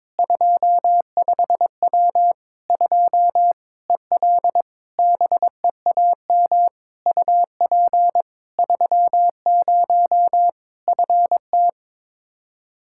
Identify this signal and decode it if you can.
Morse code, heard as a keyed tone, about 22 wpm, 700 Hz. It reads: 25W 2 EL BEAM UP 30 FT
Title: Practice QSO - Exchange 2 - Info - 22wpm